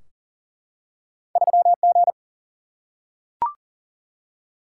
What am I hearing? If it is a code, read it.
Morse code, heard as a keyed tone, about 40 wpm, 700 Hz. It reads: 3G
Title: Single-Letter Single-Number 40wpm